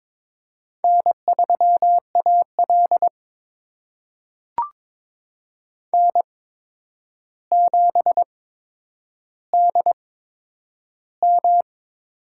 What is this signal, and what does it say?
Morse code, heard as a keyed tone, about 22 wpm, spaced as for 9 wpm, 700 Hz. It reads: N3AL N7DM
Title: USA Call Signs - 4 Characters - 8x Character Spacing - 22wpm